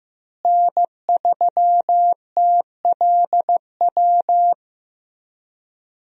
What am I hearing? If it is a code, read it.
Morse code, heard as a keyed tone, about 15 wpm, 700 Hz. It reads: N3TLW
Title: USA Call Signs - 5 Characters - Repeated 3x - 15wpm